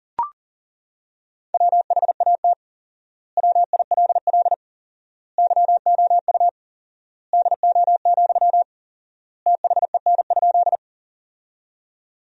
Morse code, heard as a keyed tone, about 40 wpm, 700 Hz. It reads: WHAT WILL YOU DO, THEN?
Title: 4x Word-Spacing - Sentences from Top 100 Words - 40wpm